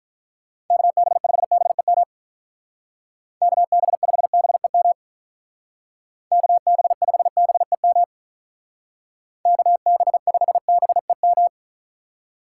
Morse code, heard as a keyed tone, about 44 wpm, 700 Hz. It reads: KB5BEM KB5BEM KB5BEM KB5BEM